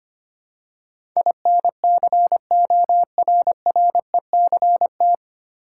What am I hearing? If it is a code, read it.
Morse code, heard as a keyed tone, about 25 wpm, 700 Hz. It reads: INCORRECT
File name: Nine Letter Words - 25wpm